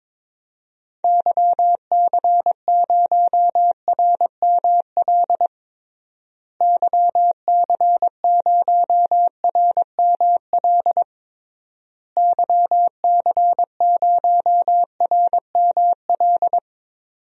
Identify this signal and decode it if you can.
Morse code, heard as a keyed tone, about 22 wpm, 700 Hz. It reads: YC0RML YC0RML YC0RML
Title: DXCC-100 Call Signs - 6 Characters - Repeated 3x - 22wpm